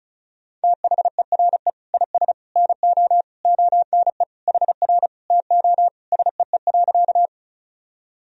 Morse code, heard as a keyed tone, about 36 wpm, 700 Hz. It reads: THERE IS NO ONE HR TO SEE.
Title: Sentences from Top 200 Words and Abbreviations 35wpm